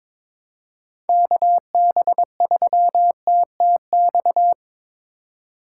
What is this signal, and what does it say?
Morse code, heard as a keyed tone, about 22 wpm, 700 Hz. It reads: KB3TTX
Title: USA Call Signs - 6 Characters - 6x Character Spacing - 22wpm